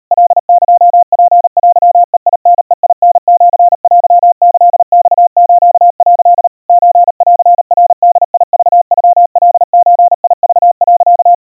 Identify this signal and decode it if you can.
Morse code (Japanese Wabun code), heard as a keyed tone, about 37 wpm, 700 Hz. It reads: カエツテベタベタシテキマス」 ソンナボクノカゾク、
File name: kawa / Morse sample / kamyu 140 cpm 700 Hz Crispy